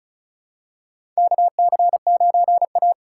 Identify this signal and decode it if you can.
Morse code, heard as a keyed tone, about 34 wpm, 700 Hz. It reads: KC9A
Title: USA Call Signs - 4 Characters - 4x Character Spacing - 35wpm